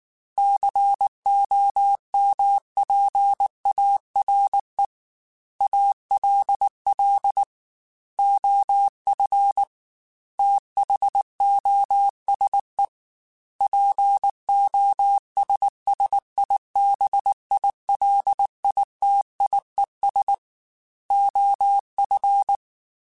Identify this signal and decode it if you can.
Morse code, heard as a keyed tone, about 19 wpm, 800 Hz. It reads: COMPARE ALL OF THOSE POSSIBILITIES OF
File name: Technical Subjects / Technical Subjects-mp3s / OR Gate @18